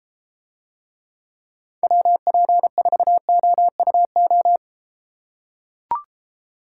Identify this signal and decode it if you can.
Morse code, heard as a keyed tone, about 33 wpm, 700 Hz. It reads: WP4OUO